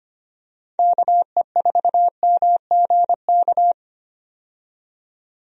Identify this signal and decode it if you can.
Morse code, heard as a keyed tone, about 25 wpm, 700 Hz. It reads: KE4MGK
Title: USA Call Signs - 6 Characters - 25wpm